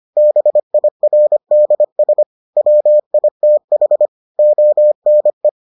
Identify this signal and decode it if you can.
Morse code, heard as a keyed tone, about 25 wpm, 600 Hz. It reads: BIRDS WITH ONE